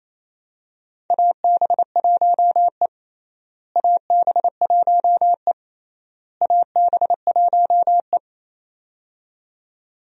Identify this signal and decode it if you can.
Morse code, heard as a keyed tone, about 28 wpm, 700 Hz. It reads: AB1E AB1E AB1E
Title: USA Call Signs - 4 Characters - Repeated 3x - 28wpm